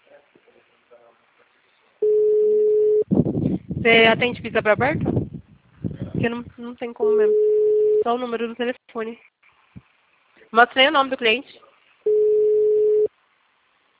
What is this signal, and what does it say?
Ringback tone (Europe, 425 Hz), with speech.
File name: out- -2049- - - .33681